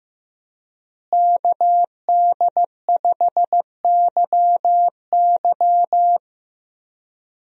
Morse code, heard as a keyed tone, about 15 wpm, 700 Hz. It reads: KD5YY
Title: USA Call Signs - 5 Characters - Repeated 2x - 15wpm